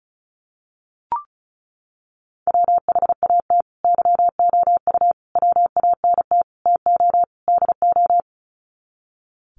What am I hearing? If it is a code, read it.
Morse code, heard as a keyed tone, about 34 wpm, 700 Hz. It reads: WHAT YOU WANT TO DO
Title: Sets of 5 Words Encore 35wpm